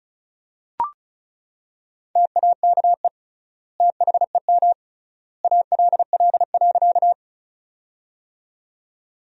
Morse code, heard as a keyed tone, about 34 wpm, 700 Hz. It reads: TAKE THEM ALL.
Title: 3x Word-Spacing - Sentences from Top 200 Words - 35wpm